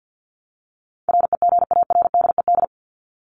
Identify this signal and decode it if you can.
Morse code, heard as a keyed tone, about 50 wpm, 700 Hz. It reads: REGARDED
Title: ED Suffix - Top 100 Words with Base Words 50wpm